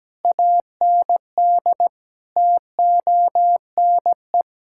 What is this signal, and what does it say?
Morse code, heard as a keyed tone, about 17 wpm, 700 Hz. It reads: AND TONE
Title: Sentences from Top 1600 Words 17wpm